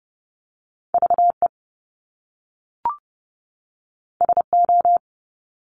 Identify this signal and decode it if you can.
Morse code, heard as a keyed tone, about 30 wpm, 700 Hz. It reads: VE SO
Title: Top Two Letter Combinations in Words 30wpm